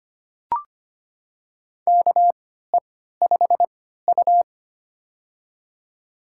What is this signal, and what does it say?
Morse code, heard as a keyed tone, about 25 wpm, spaced as for 14 wpm, 700 Hz. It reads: KE5U
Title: USA Call Signs - 4 Characters - 3x Character Spacing - 25wpm